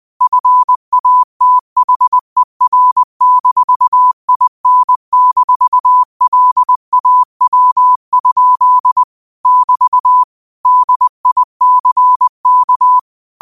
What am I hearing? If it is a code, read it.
Morse code, heard as a keyed tone, about 20 wpm, 1000 Hz. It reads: FATHER-IN-LAW? <BT> DICK
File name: qotdfiles / 20wpm